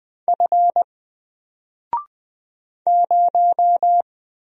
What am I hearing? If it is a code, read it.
Morse code, heard as a keyed tone, about 20 wpm, 700 Hz. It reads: F 0
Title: Single Character (Letter:Number:Punctuation:Prosign) Alt 20wpm